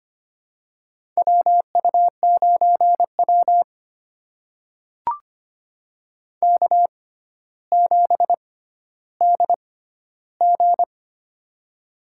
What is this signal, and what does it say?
Morse code, heard as a keyed tone, about 25 wpm, spaced as for 13 wpm, 700 Hz. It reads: WU9W K7DG
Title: USA Call Signs - 6x Character Spacing - 25wpm